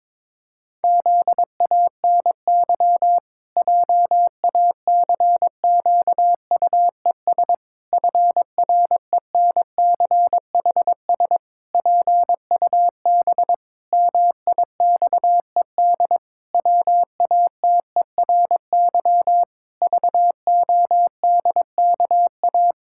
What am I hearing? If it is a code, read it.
Morse code, heard as a keyed tone, about 22 wpm, 700 Hz. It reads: ZANY JACQUES FRENCHS PUB MIXED WATERY VODKA
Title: Pangram Challenge v2 - 22wpm